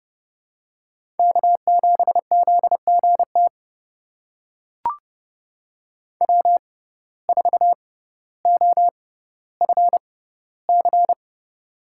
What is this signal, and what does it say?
Morse code, heard as a keyed tone, about 31 wpm, spaced as for 15 wpm, 700 Hz. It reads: K7ZGT W4OFC
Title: USA Call Signs - 5 Characters - 6x Character Spacing - 30wpm